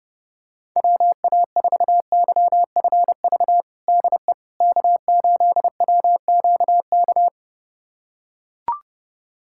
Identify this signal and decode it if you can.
Morse code, heard as a keyed tone, about 32 wpm, 700 Hz. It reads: WA4YFV DE K8WQK
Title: USA Callsign DE Callsign 30wpm